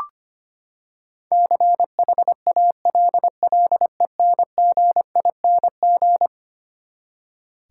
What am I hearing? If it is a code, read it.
Morse code, heard as a keyed tone, about 25 wpm, 700 Hz. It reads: CHALLENGING
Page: morse-code-ninja-practice-25wpm